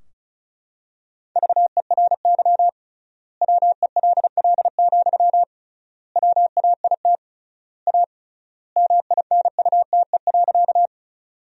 Morse code, heard as a keyed tone, about 36 wpm, 700 Hz. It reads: VERY WELL, WAIT A MINUTE.
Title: 3x Word-Spacing - Sentences from Top 500 Words - 35wpm